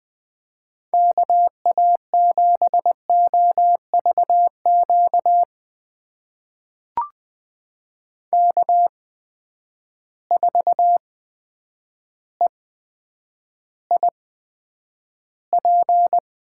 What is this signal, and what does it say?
Morse code, heard as a keyed tone, about 20 wpm, spaced as for 9 wpm, 700 Hz. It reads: KA7OVQ K4EIP
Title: USA Call Signs - 8x Character Spacing - 20wpm